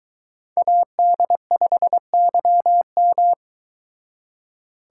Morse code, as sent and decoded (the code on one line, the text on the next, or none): .- -.. ..... -.-- --
AD5YM